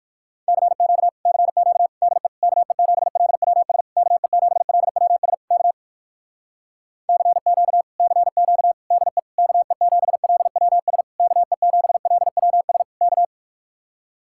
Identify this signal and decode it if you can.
Morse code, heard as a keyed tone, about 59 wpm, 700 Hz. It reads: CQ CQ DE KE7LWS KE7LWS K CQ CQ DE KE7LWS KE7LWS K